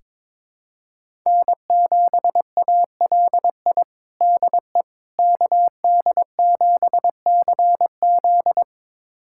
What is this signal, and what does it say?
Morse code, heard as a keyed tone, about 22 wpm, 700 Hz. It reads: N7ALI DE KD7CZ
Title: USA Callsign DE Callsign - 5 Characters - 22wpm